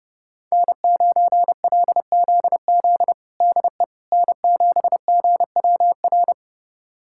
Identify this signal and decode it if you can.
Morse code, heard as a keyed tone, about 30 wpm, 700 Hz. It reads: N9LZZ DE N7GWR
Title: USA Callsign DE Callsign - 5 Characters - 30wpm